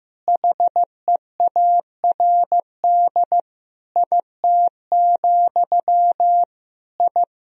Morse code, heard as a keyed tone, about 15 wpm, 700 Hz. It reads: HEARD IT, I
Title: Sentences from Top 1000 Words - Review Part 3 - 15wpm